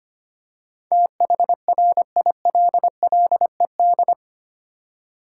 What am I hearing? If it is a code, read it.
Morse code, heard as a keyed tone, about 25 wpm, 700 Hz. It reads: THRILLED
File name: Eight Letter Words - 25wpm